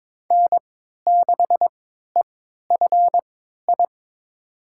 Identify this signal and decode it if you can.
Morse code, heard as a keyed tone, about 22 wpm, spaced as for 13 wpm, 700 Hz. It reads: N6EFI
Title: USA Call Signs - 5 Characters - 3x Character Spacing - 22wpm